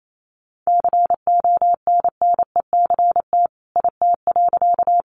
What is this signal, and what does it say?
Morse code, heard as a keyed tone, about 28 wpm, 700 Hz. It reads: CONNECT IT.
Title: Sentences from Top 1500 Words 28wpm